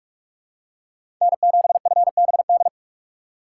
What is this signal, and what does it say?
Morse code, heard as a keyed tone, about 47 wpm, 700 Hz. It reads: N7FBD
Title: USA Call Signs - 5 Characters - 2x Character Spacing - 45wpm